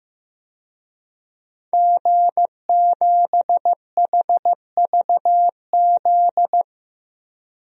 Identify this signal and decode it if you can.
Morse code, heard as a keyed tone, about 15 wpm, 700 Hz. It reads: G7HVZ
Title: DXCC-100 Call Signs - 5 Characters - 15wpm